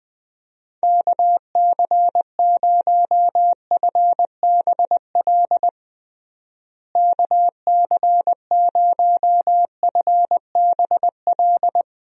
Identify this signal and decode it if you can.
Morse code, heard as a keyed tone, about 20 wpm, 700 Hz. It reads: KC0FBL KC0FBL
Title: USA Call Signs - 6 Characters - Repeated 2x - 20wpm